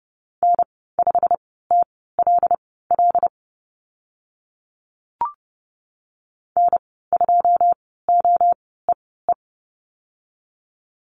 Morse code, heard as a keyed tone, about 30 wpm, spaced as for 17 wpm, 700 Hz. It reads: N5TLL N2OEE
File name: Lesson 15g - Number 5 - Call Signs 30wpm